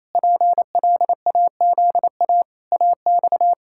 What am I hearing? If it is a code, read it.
Morse code, heard as a keyed tone, about 27 wpm, 700 Hz. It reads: PLAZA AX
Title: Pangram Challenge v2 - 28wpm